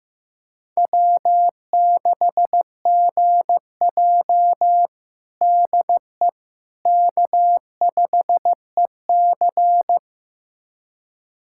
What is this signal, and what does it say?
Morse code, heard as a keyed tone, about 16 wpm, 700 Hz. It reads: W6GJ DE K5EC
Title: USA Callsign DE Callsign - 4 Characters - 15wpm